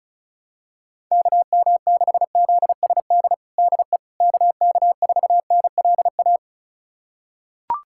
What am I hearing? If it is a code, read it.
Morse code, heard as a keyed tone, about 34 wpm, 700 Hz. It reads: KM6ZSD DE KK4NLA
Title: USA Callsign DE Callsign - 6 Characters - 35wpm